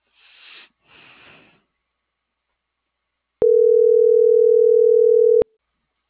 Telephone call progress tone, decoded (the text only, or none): ringback tone